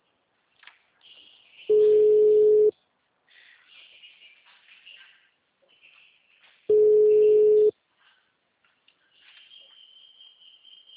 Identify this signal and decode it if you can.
Ringback tone (Europe, 425 Hz).